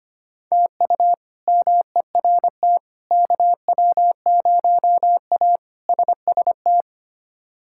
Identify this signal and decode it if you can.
Morse code, heard as a keyed tone, about 25 wpm, 700 Hz. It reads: TU MERT KW0A SST
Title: K1USN SST QSO Practice - 25wpm